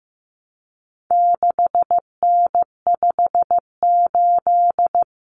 Morse code, heard as a keyed tone, about 15 wpm, 700 Hz. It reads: 6N58
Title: Common Cut Numbers - 4-Digit 15wpm